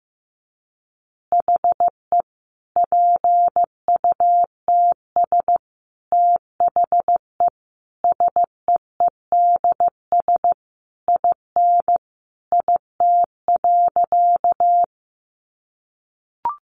Morse code, heard as a keyed tone, about 14 wpm, 700 Hz. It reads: HE PUTS THE SEEDS IN IT.
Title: Easy Phrases 15wpm